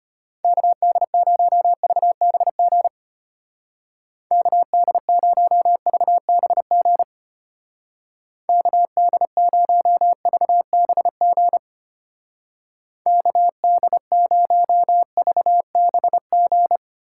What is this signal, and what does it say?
Morse code, heard as a keyed tone, about 32 wpm, 700 Hz. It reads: KD0VBG KD0VBG KD0VBG KD0VBG